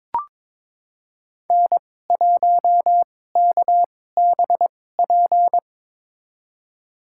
Morse code, heard as a keyed tone, about 21 wpm, spaced as for 16 wpm, 700 Hz. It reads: N1KBP
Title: USA Call Signs - 2x Character Spacing - 22wpm